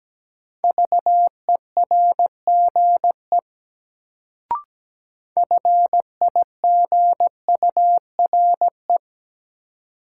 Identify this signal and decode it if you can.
Morse code, heard as a keyed tone, about 17 wpm, 700 Hz. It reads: VERGE FIGURE
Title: Sets of 4 Words - New Word Drill - 17wpm